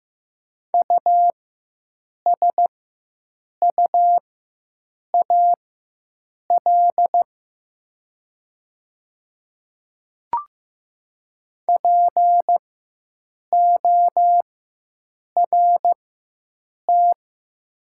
Morse code, heard as a keyed tone, about 15 wpm, spaced as for 7 wpm, 700 Hz. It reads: USUAL PORT
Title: 4x Character Spacing - Top 500 Words - 15wpm